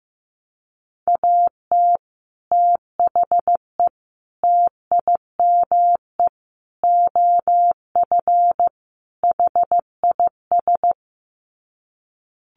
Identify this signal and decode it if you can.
Morse code, heard as a keyed tone, about 15 wpm, 700 Hz. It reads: AT THE TIME OF HIS
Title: Sets of 5 Words Encore2 15wpm